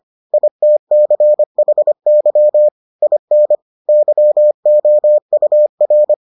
Morse code, heard as a keyed tone, about 25 wpm, 600 Hz. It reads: ITCHY IN YOUR